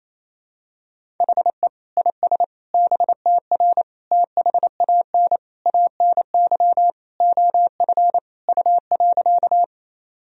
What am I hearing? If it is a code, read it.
Morse code, heard as a keyed tone, about 28 wpm, 700 Hz. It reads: HE IS BTR THAN ANY OF U.
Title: Sentences from Top 400 Words and Abbreviations 28wpm